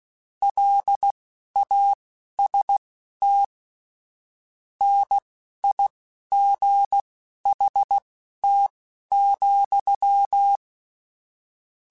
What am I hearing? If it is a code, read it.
Morse code, heard as a keyed tone, about 16 wpm, spaced as for 12 wpm, 800 Hz. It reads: LAST NIGHT,
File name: Weather Sets / Weather Sets-mp3s / Weather 4 17 @10